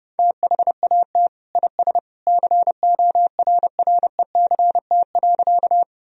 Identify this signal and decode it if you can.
Morse code, heard as a keyed tone, about 32 wpm, 700 Hz. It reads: THAT IS CORRECT.
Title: Sentences from Top 1000 Words - Review Part 2 - 30wpm